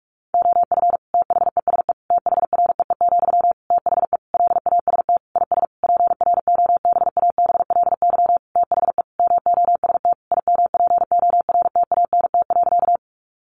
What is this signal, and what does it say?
Morse code, heard as a keyed tone, about 44 wpm, 700 Hz. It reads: OF THESE THREE, THE LAST IS PROBABLY THE MOST IMPORTANT.
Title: Sentences from Top 700 Words 45wpm